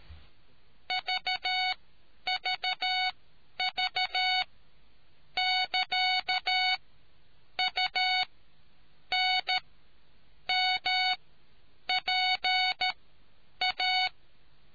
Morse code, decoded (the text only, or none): VVV <KA> UNMPA